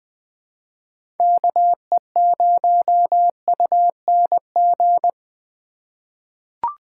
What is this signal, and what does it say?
Morse code, heard as a keyed tone, about 20 wpm, 700 Hz. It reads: KE0UNG
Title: USA Call Signs - 6 Characters - 4x Character Spacing - 20wpm